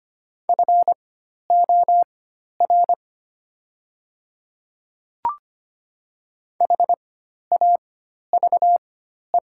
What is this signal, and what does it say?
Morse code, heard as a keyed tone, about 25 wpm, spaced as for 12 wpm, 700 Hz. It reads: FOR HAVE
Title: 4x Character Spacing - Top 300 Words - 25wpm